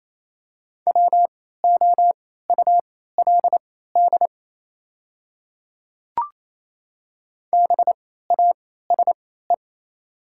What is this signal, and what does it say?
Morse code, heard as a keyed tone, about 27 wpm, spaced as for 16 wpm, 700 Hz. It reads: WOULD BASE
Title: 3x Character Spacing - Top 300 Words - 28wpm